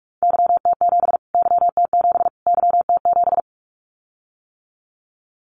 Morse code, heard as a keyed tone, about 44 wpm, 700 Hz. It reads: YT7 YT7 YT7
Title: Callsign Prefixes for Top 25 DXCC - Repeated 3x - 45wpm